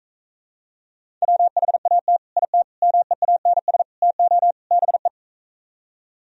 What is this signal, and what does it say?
Morse code, heard as a keyed tone, about 45 wpm, 700 Hz. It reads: WHAT IT MEANS TO BE